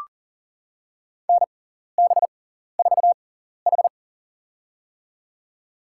Morse code, heard as a keyed tone, about 40 wpm, spaced as for 14 wpm, 700 Hz. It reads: NB4H
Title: USA Call Signs - 4 Characters - 6x Character Spacing - 40wpm